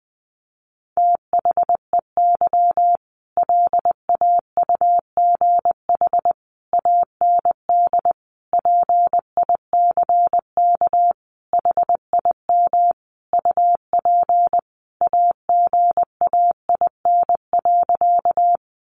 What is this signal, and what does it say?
Morse code, heard as a keyed tone, about 20 wpm, 700 Hz. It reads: THEY LAUGH AND PICK HIM UP AGAIN.
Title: Sentences from Top 1000 Words 20wpm